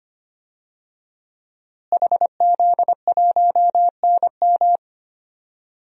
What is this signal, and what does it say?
Morse code, heard as a keyed tone, about 25 wpm, 700 Hz. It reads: HZ1NM
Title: DXCC-100 Call Signs - 5 Characters - Repeated 2x - 25wpm